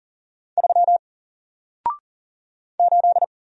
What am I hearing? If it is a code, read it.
Morse code, heard as a keyed tone, about 40 wpm, 700 Hz. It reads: 3 8
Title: Single Letter-Number 40wpm